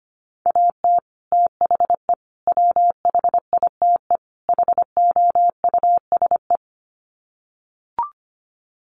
Morse code, heard as a keyed tone, about 25 wpm, 700 Hz. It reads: AT THE WHITE HOUSE
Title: Sets of 4 Words 25wpm